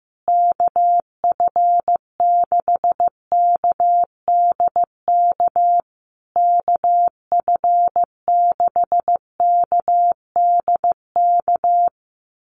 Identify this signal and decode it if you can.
Morse code, heard as a keyed tone, about 15 wpm, 700 Hz. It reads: KF6KDK KF6KDK
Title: USA Call Signs Repeated 3x 15wpm